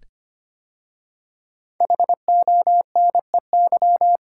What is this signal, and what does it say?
Morse code, heard as a keyed tone, about 25 wpm, 700 Hz. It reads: HONEY
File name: 5 Letter Words - First 3 Letters Given - 25wpm